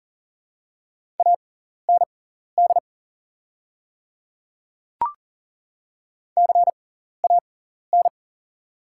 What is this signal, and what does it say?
Morse code, heard as a keyed tone, about 40 wpm, spaced as for 14 wpm, 700 Hz. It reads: AND CAN
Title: 6x Character Spacing - Top 200 Words - 40wpm